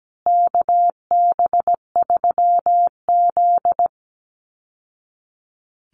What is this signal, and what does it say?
Morse code, heard as a keyed tone, about 17 wpm, 700 Hz. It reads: KB3Z
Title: Call Signs Repeated 2x - 17wpm